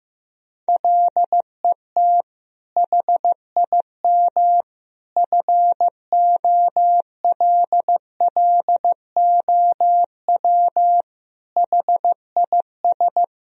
Morse code, heard as a keyed tone, about 15 wpm, 700 Hz. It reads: LET HIM FOLLOW HIS
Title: Sentences from Top 1000 Words - Review Part 1 - 15wpm